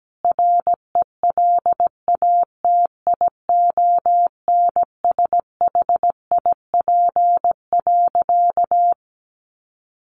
Morse code, heard as a keyed tone, about 17 wpm, 700 Hz. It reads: RELATIONSHIP.
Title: Sentences from Top 1100 Words 17wpm